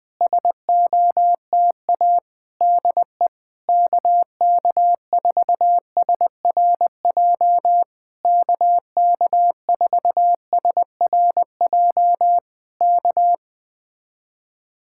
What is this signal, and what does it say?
Morse code, heard as a keyed tone, about 20 wpm, 700 Hz. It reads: SOTA DE KK4SRJ KK4SRJ K
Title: Speed-Racing Common QSO Phrases - 17wpm